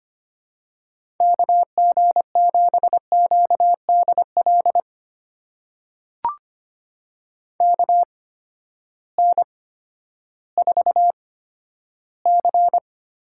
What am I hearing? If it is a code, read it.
Morse code, heard as a keyed tone, about 25 wpm, spaced as for 12 wpm, 700 Hz. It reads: KG7QDL KN4C
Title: USA Call Signs - 6 Characters - 8x Character Spacing - 25wpm